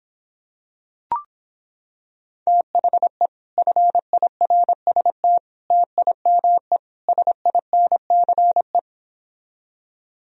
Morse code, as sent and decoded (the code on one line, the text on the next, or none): - .... / ..-. .. .-. ... - / - .. -- . / ... .. -. -.-. .
THE FIRST TIME SINCE